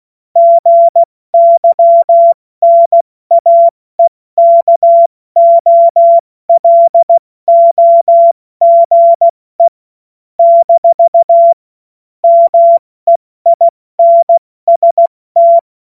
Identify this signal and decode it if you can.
Morse code, heard as a keyed tone, about 16 wpm, 680 Hz. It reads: GYNAEKOLOGE - MEINST
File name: Morsekurs / Witze / Witz 16 April 2024 70 F80